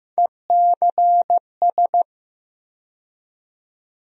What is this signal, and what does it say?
Morse code, heard as a keyed tone, about 15 wpm, 700 Hz. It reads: ECS
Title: USA Call Sign Postfixes - 3 Letters 15wpm